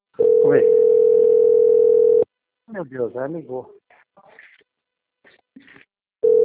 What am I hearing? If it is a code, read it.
Ringback tone (North America), 440 + 480 Hz, with speech.
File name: out- - - - - .15253